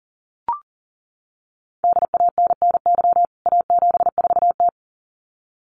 Morse code, heard as a keyed tone, about 40 wpm, 700 Hz. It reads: DANNY A74T
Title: CWT Exchange 40wpm